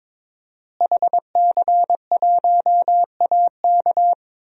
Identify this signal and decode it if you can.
Morse code, heard as a keyed tone, about 21 wpm, 700 Hz. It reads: HC1AK
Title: DXCC-100 Call Signs - 5 Characters - Repeated 2x - 22wpm